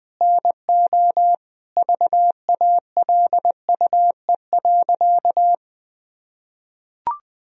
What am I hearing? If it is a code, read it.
Morse code, heard as a keyed tone, about 20 wpm, 700 Hz. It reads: NO VALUE.
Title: Sentences from Top 1000 Words - Review Part 4 - 20wpm